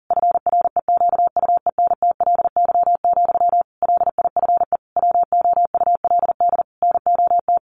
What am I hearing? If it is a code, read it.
Morse code, heard as a keyed tone, about 40 wpm, 700 Hz. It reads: FREQUENTLY, LIFE WOULD NOT
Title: Sentences from Top 1300 Words 40wpm